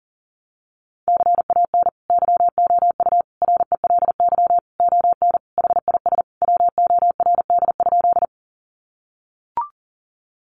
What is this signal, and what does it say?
Morse code, heard as a keyed tone, about 40 wpm, 700 Hz. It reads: CAN YOU RELY ON HIS WORD?
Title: Sentences from Top 1200 Words 40wpm